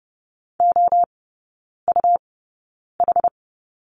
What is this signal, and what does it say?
Morse code, heard as a keyed tone, about 30 wpm, spaced as for 9 wpm, 700 Hz. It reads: OUH
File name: Lesson 17e - Letter u - Three Characters 30wpm